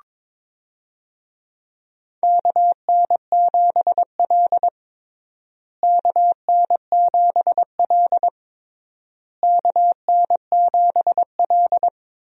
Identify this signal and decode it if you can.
Morse code, heard as a keyed tone, about 22 wpm, 700 Hz. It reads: KN7L KN7L KN7L
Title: USA Call Signs - 4 Characters - Repeated 3x - 22wpm